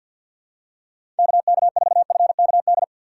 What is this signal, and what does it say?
Morse code, heard as a keyed tone, about 47 wpm, 700 Hz. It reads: KK4FKD
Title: USA Call Signs - 6 Characters - 3x Character Spacing - 50wpm